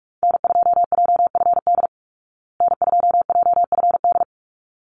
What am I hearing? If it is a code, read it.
Morse code, heard as a keyed tone, about 44 wpm, 700 Hz. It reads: N2JFD N2JFD
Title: Call Signs Repeated 3x - 45wpm